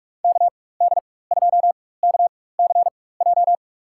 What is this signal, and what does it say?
Morse code, heard as a keyed tone, about 46 wpm, spaced as for 21 wpm, 700 Hz. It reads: KD2KCJ
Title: USA Call Signs - 4x Character Spacing - 45wpm